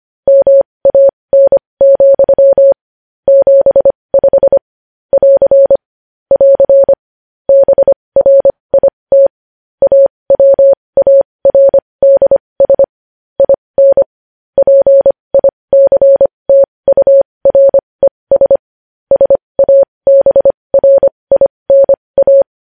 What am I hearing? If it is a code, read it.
Morse code, heard as a keyed tone, about 25 wpm, spaced as for 20 wpm, 550 Hz. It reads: MAN, 75 <AR> <AR> BRIT AWARDS IN PICTURES SABRINA
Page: news-headlines-in-morse-code-at-20-wpm